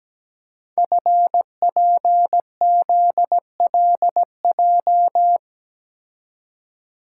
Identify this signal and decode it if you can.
Morse code, heard as a keyed tone, about 17 wpm, 700 Hz. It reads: FPZLJ
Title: Code Groups - 5 Letter:Numbers - 17wpm